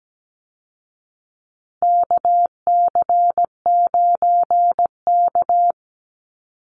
Morse code, heard as a keyed tone, about 17 wpm, 700 Hz. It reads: KC9K
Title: Call Signs Repeated 2x - 17wpm